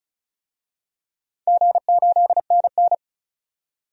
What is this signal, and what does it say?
Morse code, heard as a keyed tone, about 34 wpm, 700 Hz. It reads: G8NN
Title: DXCC-100 Call Signs - 4 Characters - 35wpm